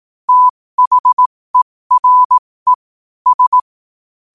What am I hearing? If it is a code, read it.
Morse code, heard as a keyed tone, about 18 wpm, spaced as for 15 wpm, 1000 Hz. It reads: THERE S
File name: qotdfiles / 15wpm